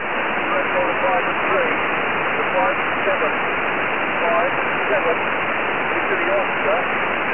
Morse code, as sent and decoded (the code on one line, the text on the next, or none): none